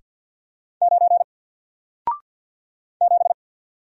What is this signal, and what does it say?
Morse code, heard as a keyed tone, about 50 wpm, 700 Hz. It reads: 9 7
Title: Single Letter-Number 50wpm